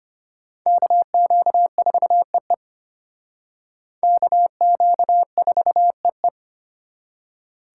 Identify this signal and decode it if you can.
Morse code, heard as a keyed tone, about 31 wpm, 700 Hz. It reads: KQ4EE KQ4EE
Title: Speed Racing - USA 2x2 Call Signs 25wpm